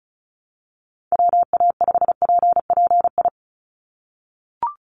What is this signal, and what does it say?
Morse code, heard as a keyed tone, about 34 wpm, 700 Hz. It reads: WA5PPI
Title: USA Call Signs 35wpm